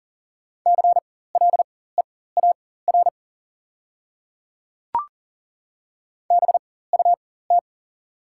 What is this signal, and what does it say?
Morse code, heard as a keyed tone, about 40 wpm, spaced as for 19 wpm, 700 Hz. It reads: CLEAR BUT
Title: 4x Character Spacing - Top 500 Words - 40wpm